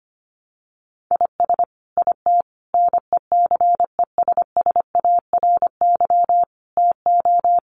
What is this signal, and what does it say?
Morse code, heard as a keyed tone, about 25 wpm, 700 Hz. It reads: IS IT NECESSARY TO
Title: Sentences from Top 1500 Words 25wpm